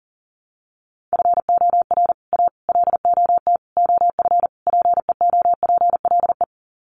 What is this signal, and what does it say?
Morse code, heard as a keyed tone, about 40 wpm, 700 Hz. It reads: FOR A LOT OF PEOPLE
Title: Sets of 5 Words Encore 40wpm